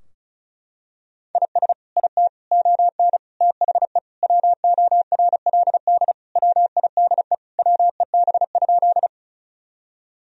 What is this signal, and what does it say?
Morse code, heard as a keyed tone, about 34 wpm, 700 Hz. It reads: IS IT ON THE WORLD WIDE WEB?
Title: Sentences from Top 1000 Words - Review Part 4 - 35wpm